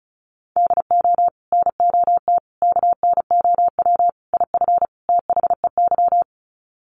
Morse code, heard as a keyed tone, about 34 wpm, 700 Hz. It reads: DO NOT KNOW IF THEY
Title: Sets of 5 Words Encore 35wpm